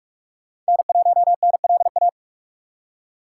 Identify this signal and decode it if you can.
Morse code, heard as a keyed tone, about 60 wpm, spaced as for 53 wpm, 700 Hz. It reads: N1NLA
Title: Speed Racing - USA 1x3 Call Signs 45wpm